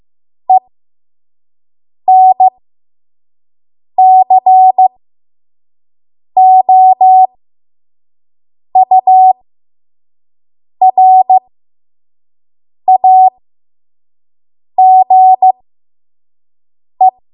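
Morse code, heard as a keyed tone, about 15 wpm, spaced as for 5 wpm, 750 Hz. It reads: ENCOURAGE
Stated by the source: 5 WPM Morse code practice.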